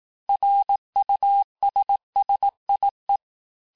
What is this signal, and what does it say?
Morse code, heard as a keyed tone, about 18 wpm, 800 Hz. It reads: RUSSIE
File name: texte-clair-vitesse-17mpm-puis-22mpm